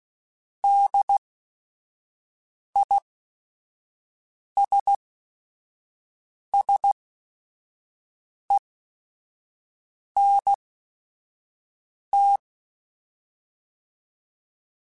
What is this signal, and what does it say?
Morse code, heard as a keyed tone, about 16 wpm, spaced as for 5 wpm, 800 Hz. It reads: DISSENT